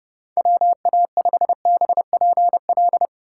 Morse code, heard as a keyed tone, about 32 wpm, 700 Hz. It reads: WA5BPL